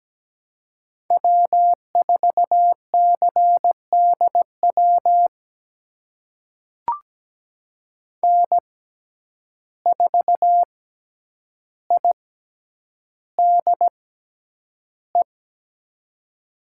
Morse code, heard as a keyed tone, about 17 wpm, spaced as for 9 wpm, 700 Hz. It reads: W4CDW N4IDE
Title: USA Call Signs - 5 Characters - 6x Character Spacing - 17wpm